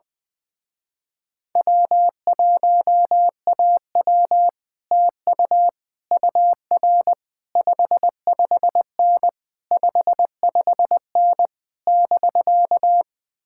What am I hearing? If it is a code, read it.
Morse code, heard as a keyed tone, about 20 wpm, 700 Hz. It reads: W1AW TU UR 55N 55N <BK>
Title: Speed-Racing Common QSO Phrases Encore - 20wpm